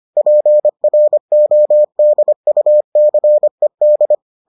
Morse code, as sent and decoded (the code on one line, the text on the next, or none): .--. .-. --- -.. ..- -.-. -..
PRODUCED